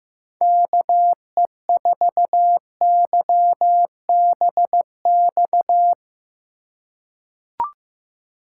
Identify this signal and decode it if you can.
Morse code, heard as a keyed tone, about 15 wpm, 700 Hz. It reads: KE4YBX
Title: USA Call Signs - 6 Characters - 3x Character Spacing - 15wpm